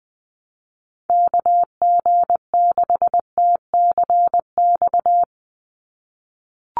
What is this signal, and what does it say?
Morse code, heard as a keyed tone, about 20 wpm, 700 Hz. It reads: KG6TCX
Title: USA Call Signs 20wpm